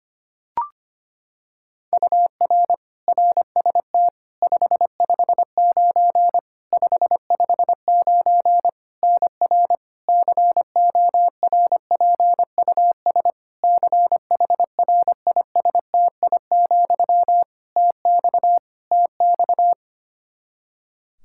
Morse code, heard as a keyed tone, about 25 wpm, 700 Hz. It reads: UR RST 559 559 NR CORPUS CHRISTI, TX TX
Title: K1USN SST QSO Practice 25wpm